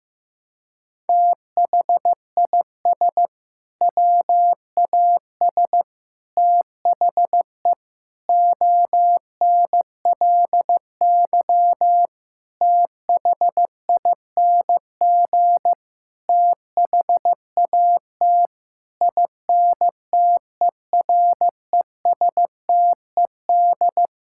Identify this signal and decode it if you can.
Morse code, heard as a keyed tone, about 15 wpm, 700 Hz. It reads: THIS WAS THE ONLY THING THAT INTERESTED
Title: Sentences from Top 1000 Words - Review Part 6 - 15wpm